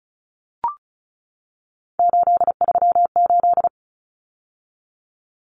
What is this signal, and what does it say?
Morse code, heard as a keyed tone, about 34 wpm, 700 Hz. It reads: 838
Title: Numbers - 3-Digit 35wpm